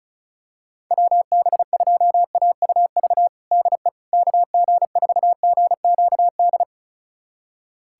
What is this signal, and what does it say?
Morse code, heard as a keyed tone, about 34 wpm, 700 Hz. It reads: WB2AUV DE KG4GQD
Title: USA Callsign DE Callsign - 6 Characters - 35wpm